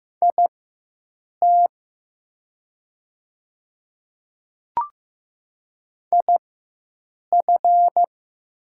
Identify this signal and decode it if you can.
Morse code, heard as a keyed tone, about 15 wpm, spaced as for 7 wpm, 700 Hz. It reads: IT IF
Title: 4x Character Spacing - Top 100 Words - 15wpm